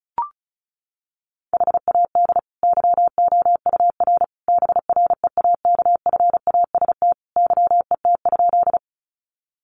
Morse code, heard as a keyed tone, about 34 wpm, 700 Hz. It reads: HAD YOUR BREAKFAST YET?
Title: Sentences from Top 1500 Words 35wpm